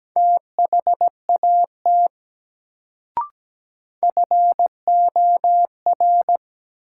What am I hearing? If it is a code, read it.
Morse code, heard as a keyed tone, about 17 wpm, 700 Hz. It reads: THAT FOR
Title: Sentences from Top 100 Words - New Word Drill - 17wpm